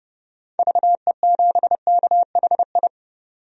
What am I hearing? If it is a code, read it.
Morse code, heard as a keyed tone, about 30 wpm, 700 Hz. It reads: VE7KHI
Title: DXCC-100 Call Signs - 6 Characters - 30wpm